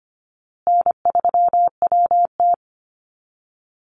Morse code, heard as a keyed tone, about 25 wpm, 700 Hz. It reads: N3WT
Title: Easy Call Signs Repeated 3x - 25wpm